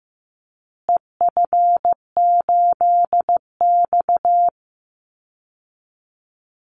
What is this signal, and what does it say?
Morse code, heard as a keyed tone, about 15 wpm, 700 Hz. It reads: EF8X
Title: Call Signs Repeated 2x - 15wpm